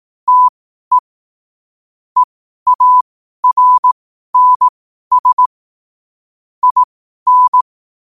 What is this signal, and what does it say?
Morse code, heard as a keyed tone, about 18 wpm, spaced as for 12 wpm, 1000 Hz. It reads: TE EARNS IN
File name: qotdfiles / 12wpm